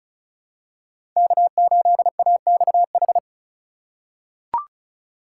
Morse code, heard as a keyed tone, about 34 wpm, 700 Hz. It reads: K8AXH
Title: USA Call Signs - 5 Characters - 4x Character Spacing - 35wpm